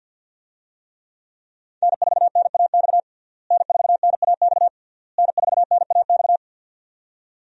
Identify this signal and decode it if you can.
Morse code, heard as a keyed tone, about 50 wpm, 700 Hz. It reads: N4NAX N4NAX N4NAX
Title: USA Call Signs - 5 Characters - Repeated 3x - 50wpm